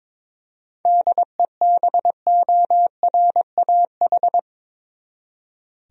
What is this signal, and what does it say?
Morse code, heard as a keyed tone, about 22 wpm, 700 Hz. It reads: DEBORAH
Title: USA Common Names - 22wpm